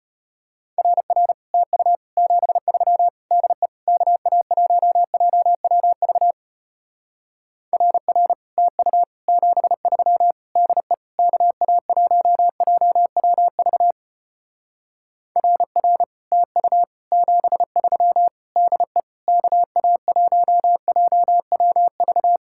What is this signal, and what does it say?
Morse code, heard as a keyed tone, about 33 wpm, 700 Hz. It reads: RR TU 73 DE KA1JWV RR TU 73 DE KA1JWV RR TU 73 DE KA1JWV